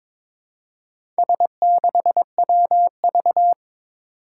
Morse code, heard as a keyed tone, about 22 wpm, 700 Hz. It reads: S6WV
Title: Four Challenging Letter-Number Combinations 22wpm